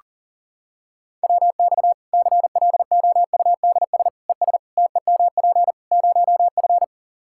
Morse code, heard as a keyed tone, about 40 wpm, 700 Hz. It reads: WX CLOUDS ES TEMP 0F
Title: Practice QSO - Exchange 2 - Info - 40wpm